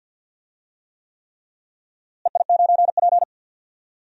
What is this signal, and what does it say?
Morse code, heard as a keyed tone, about 50 wpm, 700 Hz. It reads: EI9P
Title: DXCC-100 Call Signs - 4 Characters - Repeated 3x - 50wpm